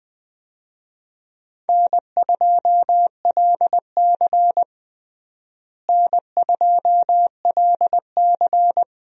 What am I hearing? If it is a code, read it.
Morse code, heard as a keyed tone, about 20 wpm, 700 Hz. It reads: N2LC N2LC
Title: USA Call Signs - 4 Characters - Repeated 3x - 20wpm